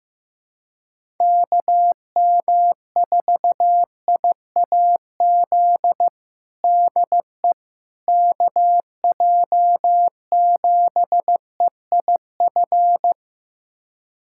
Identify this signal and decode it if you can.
Morse code, heard as a keyed tone, about 15 wpm, 700 Hz. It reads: KM4IAZ DE KJ7EIF
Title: USA Callsign DE Callsign 15wpm